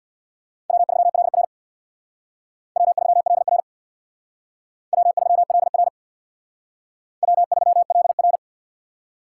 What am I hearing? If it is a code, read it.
Morse code, heard as a keyed tone, about 61 wpm, 700 Hz. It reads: W3LR W3LR W3LR W3LR
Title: Speed Racing - USA 1x2 Call Signs 50wpm